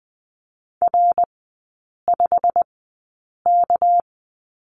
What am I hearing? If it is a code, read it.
Morse code, heard as a keyed tone, about 20 wpm, spaced as for 8 wpm, 690 Hz. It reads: R5K
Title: 212 Lesson 34e - Number 8 - Three Characters 20wpm